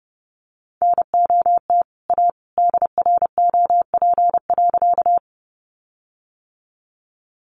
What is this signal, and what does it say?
Morse code, heard as a keyed tone, about 28 wpm, 700 Hz. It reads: NOT A DROP.
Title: Sentences from Top 1400 Words 30wpm